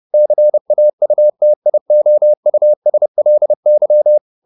Morse code, heard as a keyed tone, about 30 wpm, 600 Hz. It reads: CAUTIOUSLY
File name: the old man and the sea / the old man and the sea 1 30wpm